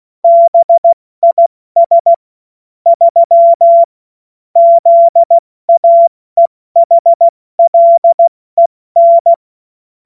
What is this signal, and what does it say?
Morse code, heard as a keyed tone, about 16 wpm, 680 Hz. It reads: BIS 3 ZAEHLEN
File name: Morsekurs / Witze / Witz 07. Dezember 2021, 70 F80